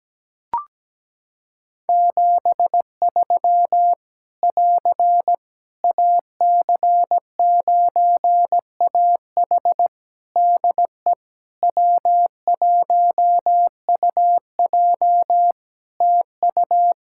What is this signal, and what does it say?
Morse code, heard as a keyed tone, about 17 wpm, 700 Hz. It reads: 73 <AR> AC9AH DE W1UJ TU
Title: K1USN SST QSO Practice 17wpm